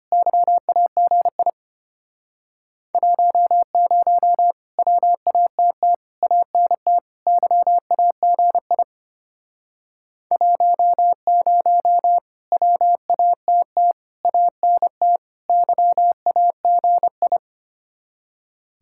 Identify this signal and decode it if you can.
Morse code, heard as a keyed tone, about 32 wpm, 690 Hz. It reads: YAGI 10 WATT ANT YAGI 10 WATT ANT YAGI